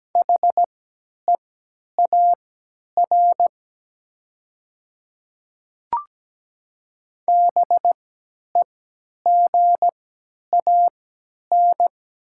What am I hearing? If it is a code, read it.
Morse code, heard as a keyed tone, about 17 wpm, spaced as for 10 wpm, 700 Hz. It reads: HEAR BEGAN
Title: 3x Character Spacing - Top 300 Words - 17wpm